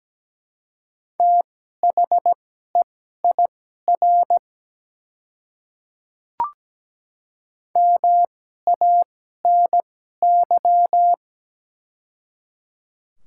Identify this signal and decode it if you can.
Morse code, heard as a keyed tone, about 17 wpm, spaced as for 12 wpm, 700 Hz. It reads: THEIR MANY
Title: 2x Character Spacing - Top 100 Words - 17wpm